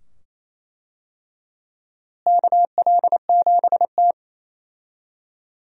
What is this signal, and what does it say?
Morse code, heard as a keyed tone, about 27 wpm, 700 Hz. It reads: KL7T
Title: USA Call Signs - 4 Characters - Repeated 2x - 28wpm